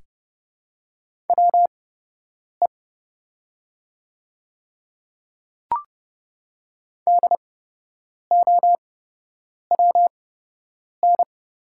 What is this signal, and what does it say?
Morse code, heard as a keyed tone, about 29 wpm, spaced as for 8 wpm, 700 Hz. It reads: WE DOWN
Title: 8x Character Spacing - Top 200 Words - 30wpm